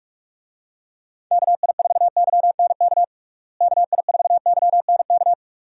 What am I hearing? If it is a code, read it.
Morse code, heard as a keyed tone, about 41 wpm, 700 Hz. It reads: KI4YNK KI4YNK
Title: USA Call Signs - 6 Characters - Repeated 3x - 45wpm